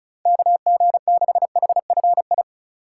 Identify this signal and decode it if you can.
Morse code, heard as a keyed tone, about 36 wpm, 700 Hz. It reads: KG6HFI
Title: USA Call Signs - 4x Character Spacing - 35wpm